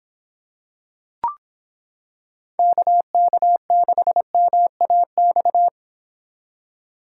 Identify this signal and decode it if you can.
Morse code, heard as a keyed tone, about 26 wpm, 700 Hz. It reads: KK6MAX